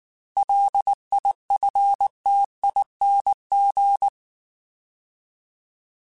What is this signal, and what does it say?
Morse code, heard as a keyed tone, about 19 wpm, 800 Hz. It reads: LIFTING
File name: Alphabetical / Alphabetical mp3s / Ls @18